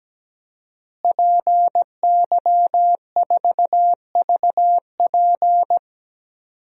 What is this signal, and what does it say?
Morse code, heard as a keyed tone, about 17 wpm, 700 Hz. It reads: PY4VP
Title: DXCC-100 Call Signs - 5 Characters - Repeated 3x - 17wpm